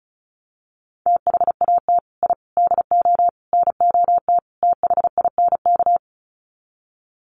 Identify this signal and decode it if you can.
Morse code, heard as a keyed tone, about 34 wpm, 700 Hz. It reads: THAT I DO NOT THINK
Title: Sets of 5 Words Encore 35wpm